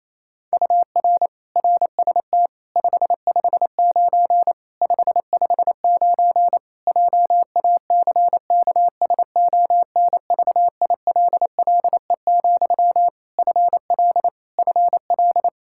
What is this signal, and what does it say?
Morse code, heard as a keyed tone, about 28 wpm, 700 Hz. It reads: UR RST 559 559 JACKSONVILLE, FL FL
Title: Practice QSO - Exchange 1 - Beginning - 28wpm